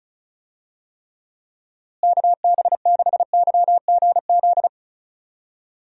Morse code, heard as a keyed tone, about 34 wpm, 700 Hz. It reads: KB6YGZ
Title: USA Call Signs - 6 Characters - Repeated 2x - 35wpm